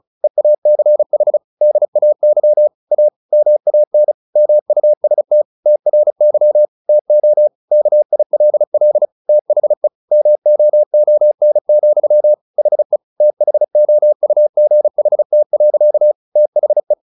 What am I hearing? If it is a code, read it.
Morse code, heard as a keyed tone, about 34 wpm, 600 Hz. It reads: EACH DAY A MAN MUST TRY TO KILL THE MOON, HE THOUGHT. THE
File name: the old man and the sea 2 35wpm